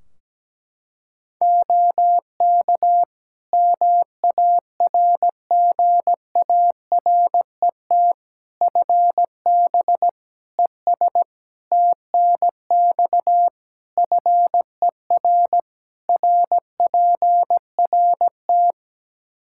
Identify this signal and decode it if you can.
Morse code, heard as a keyed tone, about 17 wpm, 700 Hz. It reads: OK MARGARET FB ES TNX FER RPRT
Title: Practice QSO - Exchange 2 - Info - 17wpm